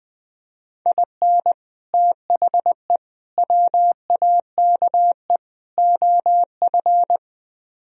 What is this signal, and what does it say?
Morse code, heard as a keyed tone, about 20 wpm, 700 Hz. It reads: IN THE WAKE OF
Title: Speed-Racing - Sets of 4 Words 17wpm